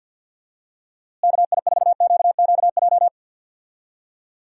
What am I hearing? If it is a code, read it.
Morse code, heard as a keyed tone, about 51 wpm, 700 Hz. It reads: KI4QQJ
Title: USA Call Signs - 6 Characters - Repeated 3x - 50wpm